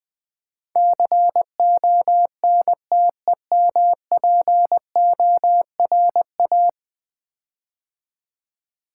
Morse code, heard as a keyed tone, about 20 wpm, 700 Hz. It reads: CONTEMPORA
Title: Build A Word - 5 Letters At A Time - 12-Letter Words - 20wpm